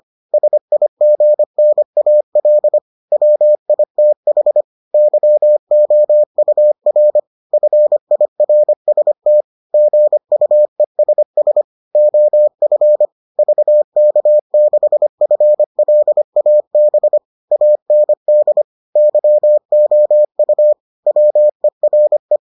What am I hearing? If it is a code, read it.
Morse code, heard as a keyed tone, about 25 wpm, 600 Hz. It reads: SIGNAL WITH YOUR FIRST GUESS OF VK6FLAB AND YOU WERE